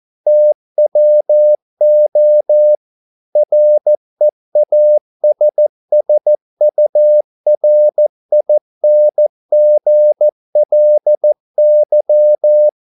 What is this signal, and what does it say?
Morse code, heard as a keyed tone, about 14 wpm, 600 Hz. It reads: TWO REASSURINGLY